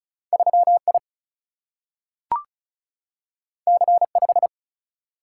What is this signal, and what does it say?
Morse code, heard as a keyed tone, about 36 wpm, 700 Hz. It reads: 3I C5
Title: Single-Letter Single-Number 35wpm